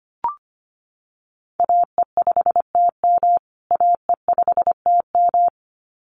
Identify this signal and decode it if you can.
Morse code, heard as a keyed tone, about 25 wpm, 700 Hz. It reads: AE5TM AE5TM
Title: USA Call Signs Repeated 2x 25wpm